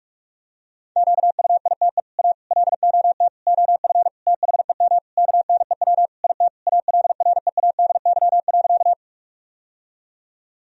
Morse code, heard as a keyed tone, about 47 wpm, 700 Hz. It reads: QUITE A LOT OF THEM KNEW IT ALREADY.
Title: Sentences from Top 1000 Words - Review Part 3 - 45wpm